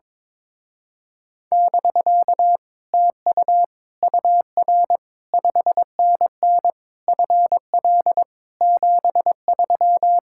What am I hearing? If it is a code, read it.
Morse code, heard as a keyed tone, about 22 wpm, 700 Hz. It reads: <BK> TU UR 5NN FL 73
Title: Speed-Racing Common QSO Phrases - 22wpm